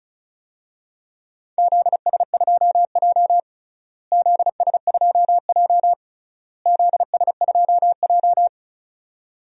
Morse code, heard as a keyed tone, about 34 wpm, 700 Hz. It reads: ZS2J ZS2J ZS2J
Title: DXCC-100 Call Signs - 4 Characters - Repeated 3x - 35wpm